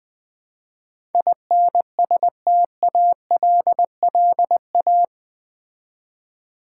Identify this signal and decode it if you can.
Morse code, heard as a keyed tone, about 20 wpm, 700 Hz. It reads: INSTALLA
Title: Build A Word - 4 Letters At A Time - 12-Letter Words - 20wpm